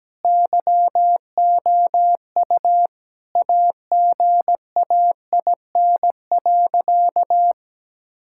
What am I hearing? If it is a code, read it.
Morse code, heard as a keyed tone, about 17 wpm, 700 Hz. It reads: YOU AGAIN.
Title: Sentences from Top 1600 Words 17wpm